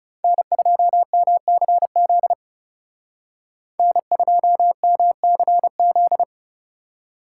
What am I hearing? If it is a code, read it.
Morse code, heard as a keyed tone, about 35 wpm, 700 Hz. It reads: N2MCZ N2MCZ
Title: Speed Racing - USA 1x3 Call Signs 30wpm